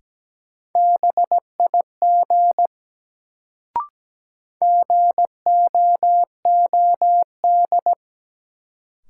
Morse code, heard as a keyed tone, about 17 wpm, 700 Hz. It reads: BIG GOOD
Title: Sentences from Top 200 Words - New Word Drill - 17wpm